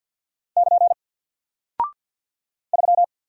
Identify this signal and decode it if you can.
Morse code, heard as a keyed tone, about 48 wpm, 700 Hz. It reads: <KN> 3
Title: Single Character (Letter:Number:Punctuation:Prosign) Alt 50wpm